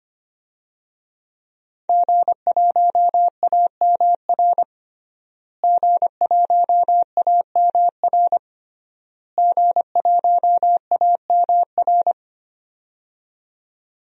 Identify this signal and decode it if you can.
Morse code, heard as a keyed tone, about 25 wpm, 700 Hz. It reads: G1AMR G1AMR G1AMR
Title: DXCC-100 Call Signs - 5 Characters - Repeated 3x - 25wpm